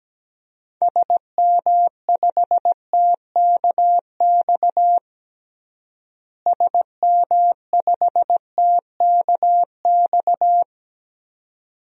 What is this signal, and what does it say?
Morse code, heard as a keyed tone, about 17 wpm, 700 Hz. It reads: SM5TKX SM5TKX
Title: DXCC-100 Call Signs - 6 Characters - Repeated 2x - 17wpm